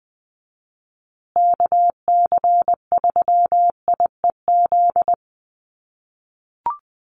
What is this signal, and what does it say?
Morse code, heard as a keyed tone, about 20 wpm, 700 Hz. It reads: KC3IEZ
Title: USA Call Signs Repeated 3x 20wpm